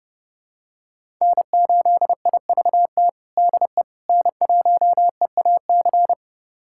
Morse code, heard as a keyed tone, about 30 wpm, 700 Hz. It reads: N8IVT DE N1EAC
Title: USA Callsign DE Callsign - 5 Characters - 30wpm